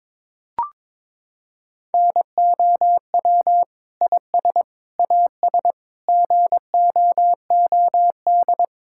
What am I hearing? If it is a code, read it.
Morse code, heard as a keyed tone, about 22 wpm, 700 Hz. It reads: NOW IS AS GOOD
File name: Sentences with 7-Word Phrases - 22wpm